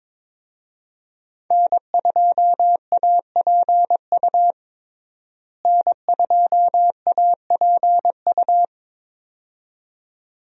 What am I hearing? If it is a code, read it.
Morse code, heard as a keyed tone, about 22 wpm, 700 Hz. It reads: N2APU N2APU
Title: USA Call Signs - 5 Characters - Repeated 2x - 22wpm